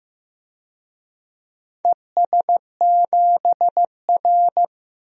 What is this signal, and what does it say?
Morse code, heard as a keyed tone, about 15 wpm, 700 Hz. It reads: ES7R
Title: DXCC-100 Call Signs - 4 Characters - Repeated 2x - 15wpm